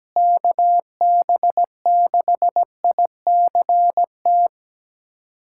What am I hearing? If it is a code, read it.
Morse code, heard as a keyed tone, about 17 wpm, 700 Hz. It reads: KB6ICT
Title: USA Callsign DE Callsign 17wpm